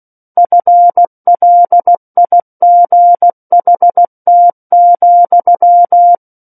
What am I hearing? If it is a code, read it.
Morse code, heard as a keyed tone, about 16 wpm, 700 Hz. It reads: FLIGHT,
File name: Book One / 07-Chapter 7 - 16 WPM